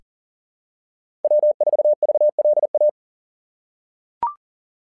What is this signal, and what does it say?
Morse code, heard as a keyed tone, about 40 wpm, 590 Hz. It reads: W4VLA
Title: SOTA QSO Practice 40wpm